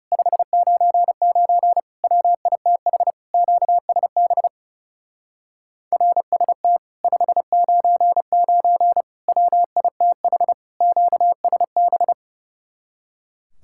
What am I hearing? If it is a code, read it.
Morse code, heard as a keyed tone, about 38 wpm, 700 Hz. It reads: 599 WITH QSB RST 599 WITH QSB